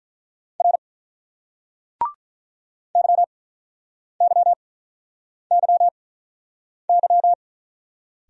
Morse code, heard as a keyed tone, about 48 wpm, spaced as for 9 wpm, 700 Hz. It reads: R YYYY